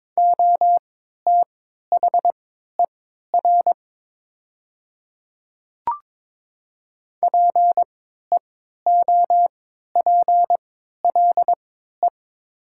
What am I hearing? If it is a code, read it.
Morse code, heard as a keyed tone, about 22 wpm, spaced as for 12 wpm, 700 Hz. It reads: OTHER PEOPLE
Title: 3x Character Spacing - Top 100 Words - 22wpm